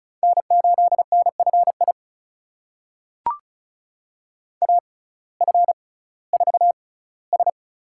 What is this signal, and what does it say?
Morse code, heard as a keyed tone, about 36 wpm, spaced as for 19 wpm, 700 Hz. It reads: N8NFI AF4S
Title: USA Call Signs - 5 Characters - 6x Character Spacing - 35wpm